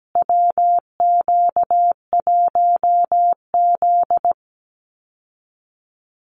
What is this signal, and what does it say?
Morse code, heard as a keyed tone, about 17 wpm, 700 Hz. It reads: WQ1Z
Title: USA Call Signs Repeated 2x 17wpm